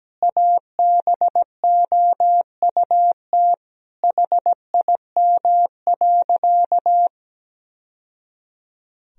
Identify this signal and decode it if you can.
Morse code, heard as a keyed tone, about 17 wpm, 700 Hz. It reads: ABOUT HIM.
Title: Sentences from Top 1000 Words - Review Part 1 - 17wpm